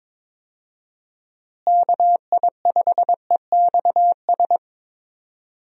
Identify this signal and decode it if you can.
Morse code, heard as a keyed tone, about 22 wpm, 700 Hz. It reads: KI5EXS
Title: USA Call Signs - 6 Characters - Repeated 3x - 22wpm